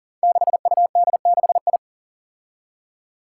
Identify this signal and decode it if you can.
Morse code, heard as a keyed tone, about 41 wpm, 700 Hz. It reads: 6UD6I
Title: Five Challenging Letter-Number Combinations 40wpm